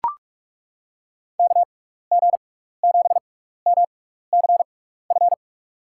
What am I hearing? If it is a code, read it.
Morse code, heard as a keyed tone, about 42 wpm, spaced as for 15 wpm, 700 Hz. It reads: KG7MCF
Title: USA Call Signs - 6x Character Spacing - 45wpm